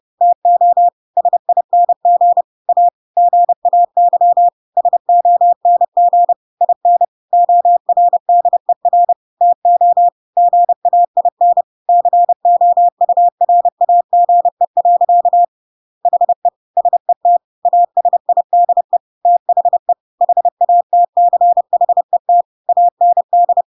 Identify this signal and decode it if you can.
Morse code, heard as a keyed tone, about 30 wpm, 700 Hz. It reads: TO SING A GAY SONG IN ORDER TO GAIN COURAGE. HE SET ASIDE THE HATCHET AND
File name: Pinocchio / Pinocchio ch1 30 30 ik1hge